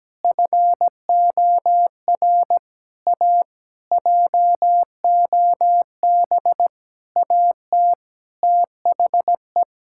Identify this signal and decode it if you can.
Morse code, heard as a keyed tone, about 17 wpm, 700 Hz. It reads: FOR A JOB AT THE
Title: Sentences from Top 1600 Words 17wpm